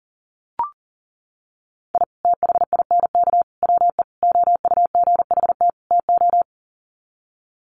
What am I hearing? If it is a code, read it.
Morse code, heard as a keyed tone, about 37 wpm, 700 Hz. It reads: I THINK WE OUGHT TO
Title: Sets of 5 Words Encore 40wpm